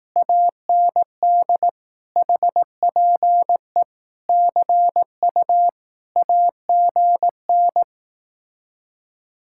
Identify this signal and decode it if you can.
Morse code, heard as a keyed tone, about 18 wpm, 700 Hz. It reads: AND HPE CU AGN
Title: Speed-Racing Common QSO Phrases Encore - 15wpm